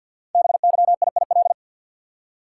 Morse code, heard as a keyed tone, about 48 wpm, 700 Hz. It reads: BYIIL
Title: Code Groups - 5 Letters - 50wpm